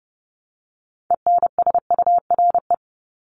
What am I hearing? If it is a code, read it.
Morse code, heard as a keyed tone, about 30 wpm, 700 Hz. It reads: ENSURE
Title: Top 1001-1250 Words 30wpm